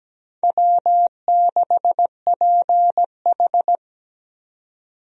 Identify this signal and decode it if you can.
Morse code, heard as a keyed tone, about 17 wpm, 700 Hz. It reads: W6PH
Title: USA Call Signs - 4 Characters - 17wpm